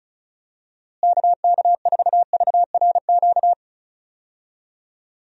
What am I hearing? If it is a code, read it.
Morse code, heard as a keyed tone, about 34 wpm, 700 Hz. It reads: KK4VRQ
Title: USA Call Signs - 6 Characters - Repeated 3x - 35wpm